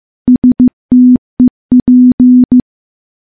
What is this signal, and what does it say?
Morse code, heard as a keyed tone, about 15 wpm, 250 Hz. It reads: STEP
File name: morse-training-15WPM